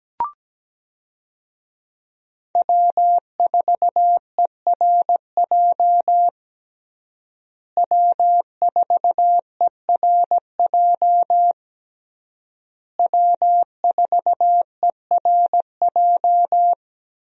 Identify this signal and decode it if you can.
Morse code, heard as a keyed tone, about 17 wpm, 700 Hz. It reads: W4ERJ W4ERJ W4ERJ
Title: USA Call Signs - 5 Characters - Repeated 3x - 17wpm